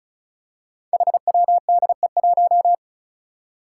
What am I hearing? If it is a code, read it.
Morse code, heard as a keyed tone, about 36 wpm, 700 Hz. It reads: HWDE1
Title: Five Challenging Letter-Number Combinations 35wpm